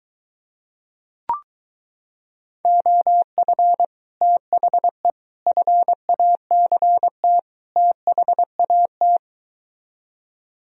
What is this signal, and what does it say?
Morse code, heard as a keyed tone, about 23 wpm, 700 Hz. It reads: OF THE FACT THAT